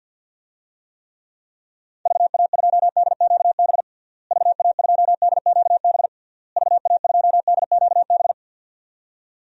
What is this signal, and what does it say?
Morse code, heard as a keyed tone, about 49 wpm, 700 Hz. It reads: VA2DQB VA2DQB VA2DQB
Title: DXCC-100 Call Signs - 6 Characters - Repeated 3x - 50wpm